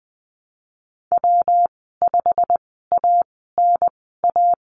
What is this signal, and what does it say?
Morse code, heard as a keyed tone, about 20 wpm, spaced as for 15 wpm, 700 Hz. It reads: W5ANA
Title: 147 Lesson 24g - Letter y - Call Signs 20wpm